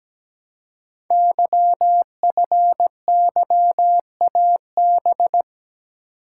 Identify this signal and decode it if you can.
Morse code, heard as a keyed tone, about 17 wpm, 700 Hz. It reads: YFYAB
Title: Code Groups - 5 Letter:Numbers - 17wpm